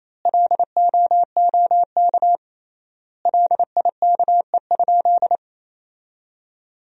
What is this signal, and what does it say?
Morse code, heard as a keyed tone, about 28 wpm, 700 Hz. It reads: LOOK LIKE?
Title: 3x Word-Spacing - Sentences from Top 100 Words - 28wpm